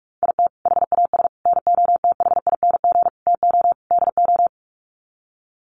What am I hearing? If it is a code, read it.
Morse code, heard as a keyed tone, about 44 wpm, 700 Hz. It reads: IT HAS NOTHING TO DO
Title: Sets of 5 Words Encore 45wpm